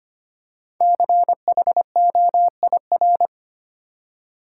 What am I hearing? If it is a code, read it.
Morse code, heard as a keyed tone, about 25 wpm, 700 Hz. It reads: CHOIR
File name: 5 Letter Words - First 3 Letters Given - 25wpm